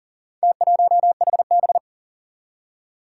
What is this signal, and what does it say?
Morse code, heard as a keyed tone, about 41 wpm, 700 Hz. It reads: T1HB
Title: Four Challenging Letter-Number Combinations 40wpm